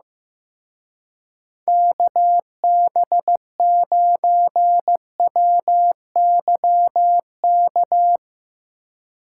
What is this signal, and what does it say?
Morse code, heard as a keyed tone, about 16 wpm, 700 Hz. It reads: KB9WYK
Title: Speed Racing - USA 2x3 Call Signs 15wpm